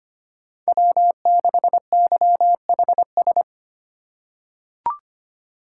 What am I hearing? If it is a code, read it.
Morse code, heard as a keyed tone, about 25 wpm, 700 Hz. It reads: W6YHS
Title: USA Call Signs - 5 Characters - 4x Character Spacing - 25wpm